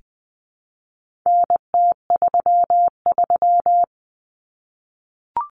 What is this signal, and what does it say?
Morse code, heard as a keyed tone, about 20 wpm, 700 Hz. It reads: NT33
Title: Common Cut Numbers - 4-Digit 20wpm